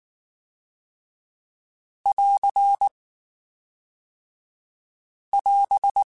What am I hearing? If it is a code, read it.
Morse code, heard as a keyed tone, about 19 wpm, 800 Hz. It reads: <AR> <AS>
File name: TH / TH mp3s / TH Prefixes 3x2 Twice @18